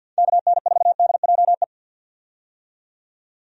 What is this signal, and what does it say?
Morse code, heard as a keyed tone, about 52 wpm, 700 Hz. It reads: KN4DJE
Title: USA Call Signs - 6 Characters - 50wpm